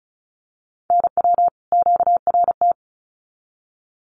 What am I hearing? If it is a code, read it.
Morse code, heard as a keyed tone, about 34 wpm, 700 Hz. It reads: NW QRT
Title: Common QSO Phrases 35wpm